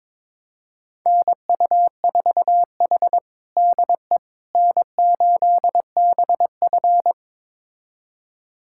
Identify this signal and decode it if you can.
Morse code, heard as a keyed tone, about 21 wpm, 700 Hz. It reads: NU4H DE N8BF
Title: USA Callsign DE Callsign - 4 Characters - 22wpm